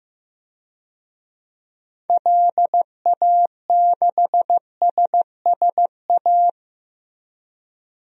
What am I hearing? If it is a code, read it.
Morse code, heard as a keyed tone, about 15 wpm, 700 Hz. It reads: LA6SSA
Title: DXCC-100 Call Signs - 6 Characters - Repeated 2x - 15wpm